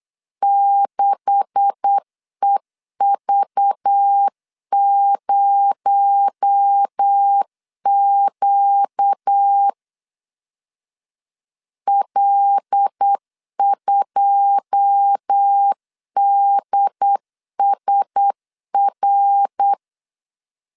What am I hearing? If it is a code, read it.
Morse code, heard as a keyed tone, about 8 wpm, 800 Hz. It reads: 6EV0Q L2DSR
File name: LEZIONI / 10 SIGNS / 10C DOT 8wpm 3649